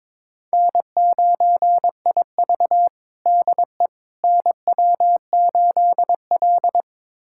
Morse code, heard as a keyed tone, about 21 wpm, 700 Hz. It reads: N9IV DE NW8L
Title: USA Callsign DE Callsign - 4 Characters - 22wpm